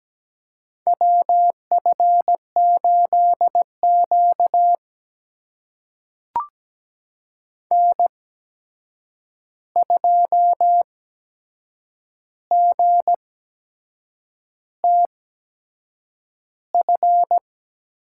Morse code, heard as a keyed tone, about 17 wpm, spaced as for 7 wpm, 700 Hz. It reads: WF8Q N2GTF
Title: USA Call Signs - 8x Character Spacing - 17wpm